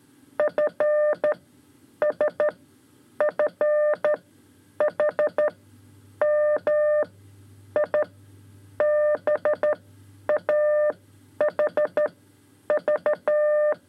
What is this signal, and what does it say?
Morse code, heard as a keyed tone, about 13 wpm, spaced as for 8 wpm, 600 Hz. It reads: FSFHM IBAHV
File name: 5 Lezione CW 2